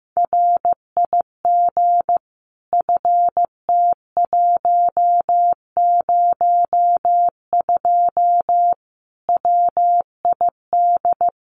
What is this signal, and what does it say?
Morse code, heard as a keyed tone, about 14 wpm, 700 Hz. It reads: RIG FT102 WID
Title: Common QSO Phrases 15wpm